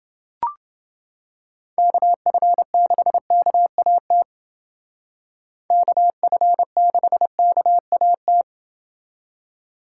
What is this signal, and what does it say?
Morse code, heard as a keyed tone, about 29 wpm, 700 Hz. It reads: KF6KAT KF6KAT